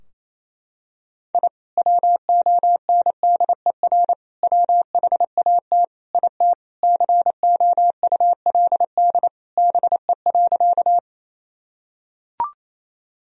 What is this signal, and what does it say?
Morse code, heard as a keyed tone, about 27 wpm, 700 Hz. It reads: I WONDER WHAT IT COULD BE.
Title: Sentences from Top 1000 Words - Review Part 3 - 28wpm